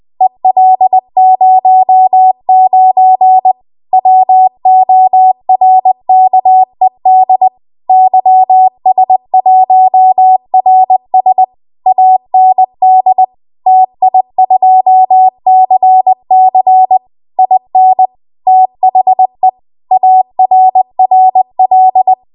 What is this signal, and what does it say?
Morse code, heard as a keyed tone, about 20 wpm, 750 Hz. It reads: EL09 WORKED YS1RS AND TI2CC IN THE ARRL
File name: Archive / 20 WPM / 20WPM